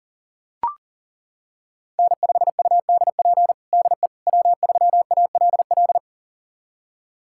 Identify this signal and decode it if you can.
Morse code, heard as a keyed tone, about 40 wpm, 700 Hz. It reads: N5UDP DE W3ALL
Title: USA Callsign DE Callsign - 5 Characters - 40wpm